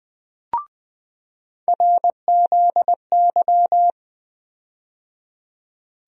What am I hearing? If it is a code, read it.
Morse code, heard as a keyed tone, about 21 wpm, 700 Hz. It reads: RZY
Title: USA Call Sign Postfixes - 3 Letters 20wpm